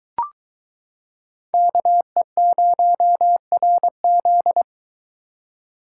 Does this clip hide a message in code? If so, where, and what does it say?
Yes: Morse code from 1.5 to 4.6 s, heard as a keyed tone, about 22 wpm, 700 Hz. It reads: KE0RZ